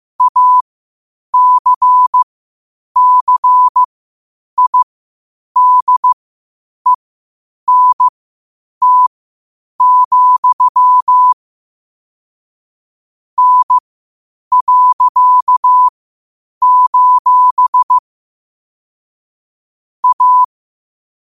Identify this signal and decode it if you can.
Morse code, heard as a keyed tone, about 15 wpm, spaced as for 8 wpm, 1000 Hz. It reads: ACCIDENT, N.: A
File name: qotdfiles / 8wpm